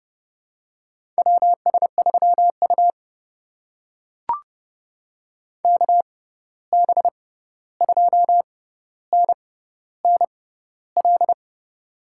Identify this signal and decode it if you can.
Morse code, heard as a keyed tone, about 31 wpm, spaced as for 14 wpm, 700 Hz. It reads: WS3U KB2NNL
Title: USA Call Signs - 6x Character Spacing - 30wpm